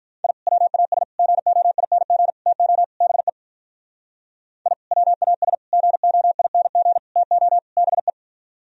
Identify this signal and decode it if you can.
Morse code, heard as a keyed tone, about 43 wpm, 700 Hz. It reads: I WAS GOING TO BE I WAS GOING TO BE
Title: Speed-Racing - Sets of 5 Words 35wpm